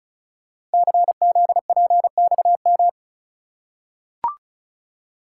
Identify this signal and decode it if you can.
Morse code, heard as a keyed tone, about 36 wpm, 700 Hz. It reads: CZPXM
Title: Code Groups - 5 Letters - 35wpm